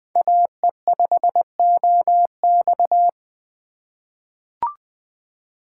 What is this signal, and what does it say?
Morse code, heard as a keyed tone, about 20 wpm, 700 Hz. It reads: AE5OX
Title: USA Call Signs - 5 Characters - 2x Character Spacing - 20wpm